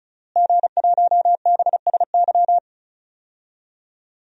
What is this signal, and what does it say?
Morse code, heard as a keyed tone, about 36 wpm, 700 Hz. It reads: G1BSY
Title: DXCC-100 Call Signs - 5 Characters - Repeated 3x - 35wpm